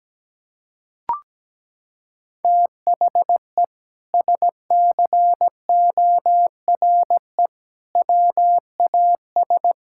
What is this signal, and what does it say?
Morse code, heard as a keyed tone, about 17 wpm, 700 Hz. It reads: THE SCORE WAS
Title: Sentences from Top 1600 Words 17wpm